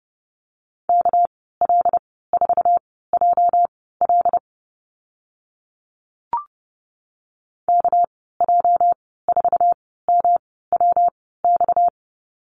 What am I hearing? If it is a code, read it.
Morse code, heard as a keyed tone, about 30 wpm, spaced as for 17 wpm, 700 Hz. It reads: KL4JL KJ4MWX
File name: Lesson 37g - Letter x - Call Signs 30wpm